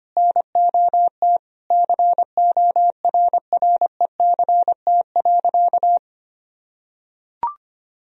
Morse code, heard as a keyed tone, about 25 wpm, 700 Hz. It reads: NOT CORRECT.
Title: Sentences from Top 1000 Words - Review Part 3 - 25wpm